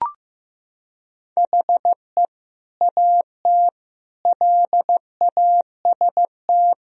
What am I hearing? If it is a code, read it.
Morse code, heard as a keyed tone, about 15 wpm, 700 Hz. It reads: HE AT LAST
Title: Sentences from Top 1000 Words - Review Part 2 - 15wpm